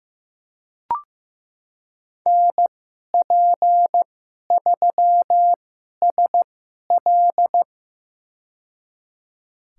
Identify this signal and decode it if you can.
Morse code, heard as a keyed tone, about 16 wpm, spaced as for 11 wpm, 700 Hz. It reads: NP3SL
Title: USA Call Signs - 5 Characters - 2x Character Spacing - 15wpm